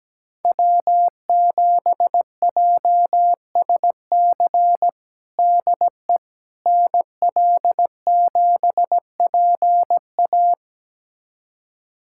Morse code, heard as a keyed tone, about 17 wpm, 700 Hz. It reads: W7JSC DE NL7PA
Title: USA Callsign DE Callsign - 5 Characters - 17wpm